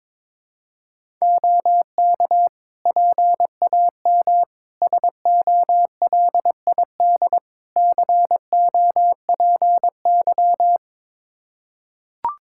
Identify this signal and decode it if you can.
Morse code, heard as a keyed tone, about 22 wpm, 700 Hz. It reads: OK PAM SOLID COPY
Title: K1USN SST QSO Practice 22wpm